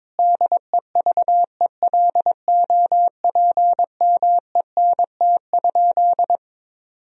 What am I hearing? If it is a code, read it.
Morse code, heard as a keyed tone, about 22 wpm, 700 Hz. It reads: DEVELOPMENT?
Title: Sentences from Top 1000 Words - Review Part 4 - 22wpm